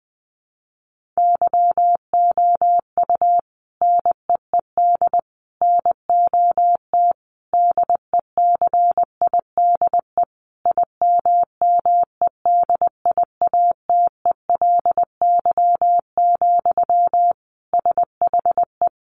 Morse code, heard as a keyed tone, about 20 wpm, 700 Hz. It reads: YOU NEED NOT DECIDE IMMEDIATELY, SHE
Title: Sentences from Top 1500 Words 20wpm